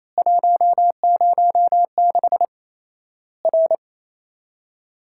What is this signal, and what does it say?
Morse code, heard as a keyed tone, about 28 wpm, 700 Hz. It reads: 106 R
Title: Worked All Europe DX Contest - QTC Exchange - 28wpm